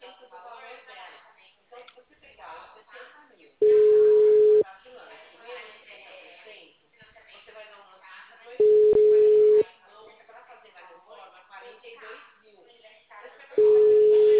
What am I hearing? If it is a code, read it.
Ringback tone (Europe, 425 Hz), with speech.